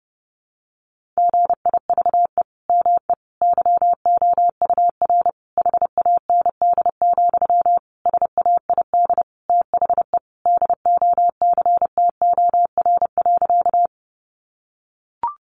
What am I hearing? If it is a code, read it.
Morse code, heard as a keyed tone, about 30 wpm, 700 Hz. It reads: GIVE ME YOUR HAND, SAID THE DOCTOR.
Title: Sentences from Top 1500 Words 30wpm